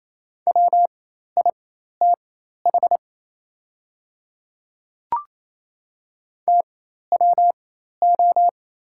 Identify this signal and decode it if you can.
Morse code, heard as a keyed tone, about 28 wpm, spaced as for 13 wpm, 700 Hz. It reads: WITH TWO
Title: 4x Character Spacing - Top 100 Words - 28wpm